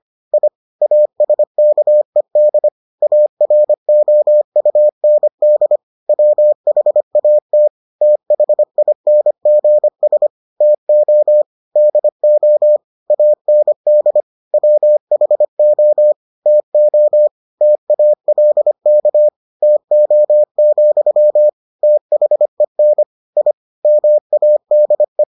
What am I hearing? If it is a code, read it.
Morse code, heard as a keyed tone, about 25 wpm, 600 Hz. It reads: I ASKED AROUND WHAT THINGS TO DO AND WHO TO TALK TO, THEN I MADE